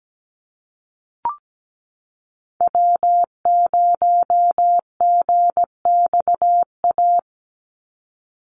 Morse code, heard as a keyed tone, about 17 wpm, 700 Hz. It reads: W0GXA
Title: Call Signs Repeated 2x - 17wpm